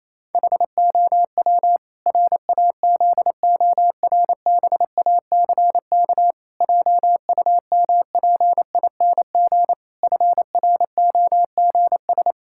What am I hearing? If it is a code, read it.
Morse code, heard as a keyed tone, about 28 wpm, 700 Hz. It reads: HOW RAZORBACK JUMPING FROGS
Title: Pangram Challenge v2 - 28wpm